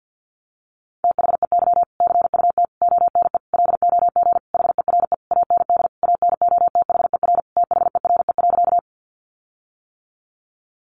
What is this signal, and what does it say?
Morse code, heard as a keyed tone, about 50 wpm, 700 Hz. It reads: THEY CUT ONE LOG HERE AND ANOTHER THERE.
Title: Sentences from Top 1400 Words 50wpm